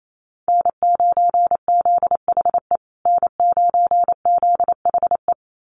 Morse code, heard as a keyed tone, about 28 wpm, 700 Hz. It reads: N9ZHE N9ZHE
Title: USA Call Signs Repeated 2x 28wpm